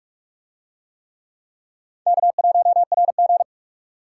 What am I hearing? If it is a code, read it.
Morse code, heard as a keyed tone, about 41 wpm, 700 Hz. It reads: K1RG